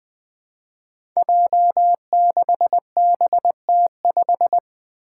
Morse code, heard as a keyed tone, about 20 wpm, 700 Hz. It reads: J6BT5
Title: Five Challenging Letter-Number Combinations 20wpm